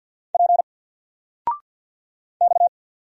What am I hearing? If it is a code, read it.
Morse code, heard as a keyed tone, about 50 wpm, 700 Hz. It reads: P X
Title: Single Letter-Number Alt 50wpm